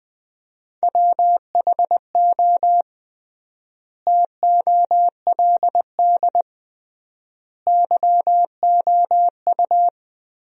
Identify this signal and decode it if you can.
Morse code, heard as a keyed tone, about 20 wpm, 700 Hz. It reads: WHO TOLD YOU
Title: 3x Word-Spacing - Sentences from Top 400 Words - 20wpm